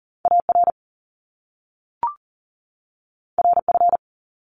Morse code, heard as a keyed tone, about 38 wpm, 700 Hz. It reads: AR RF
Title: Top Two Letter Combinations in Words 40wpm